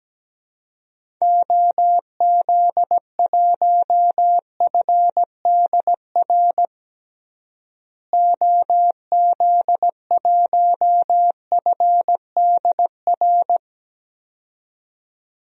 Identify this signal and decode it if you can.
Morse code, heard as a keyed tone, about 17 wpm, 700 Hz. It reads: OZ1FDR OZ1FDR
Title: DXCC-100 Call Signs - 6 Characters - Repeated 2x - 17wpm